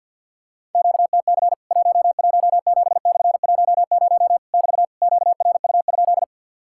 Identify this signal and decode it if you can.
Morse code, heard as a keyed tone, about 50 wpm, 700 Hz. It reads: QTC 117/10 <BT> QRU?
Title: Worked All Europe DX Contest - QTC Exchange with Repeats - 50wpm